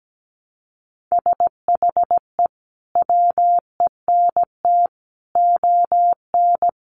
Morse code, heard as a keyed tone, about 17 wpm, 700 Hz. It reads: SHE WENT ON
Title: Sentences from Top 800 Words 17wpm